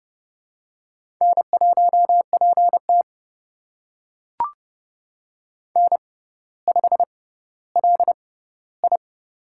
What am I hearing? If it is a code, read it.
Morse code, heard as a keyed tone, about 29 wpm, spaced as for 15 wpm, 700 Hz. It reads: N1PT N5LI
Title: USA Call Signs - 6x Character Spacing - 30wpm